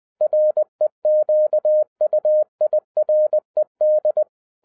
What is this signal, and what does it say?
Morse code, heard as a keyed tone, about 20 wpm, 610 Hz. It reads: REQUIRED
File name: 2011-07-mp3 / n03 KB0IZA 20wpm